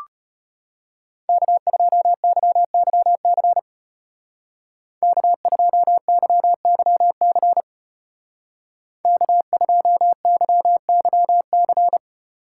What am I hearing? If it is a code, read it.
Morse code, heard as a keyed tone, about 31 wpm, 700 Hz. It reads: K2YYC K2YYC K2YYC
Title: Speed Racing - USA 1x3 Call Signs 25wpm